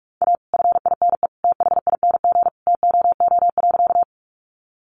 Morse code, heard as a keyed tone, about 44 wpm, 700 Hz. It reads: A FINE THING TOO.
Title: Sentences from Top 500 Words 45wpm